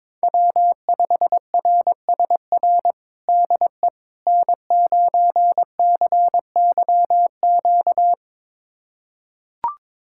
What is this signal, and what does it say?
Morse code, heard as a keyed tone, about 22 wpm, 700 Hz. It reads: W5RSR DE N9CYQ
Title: USA Callsign DE Callsign - 5 Characters - 22wpm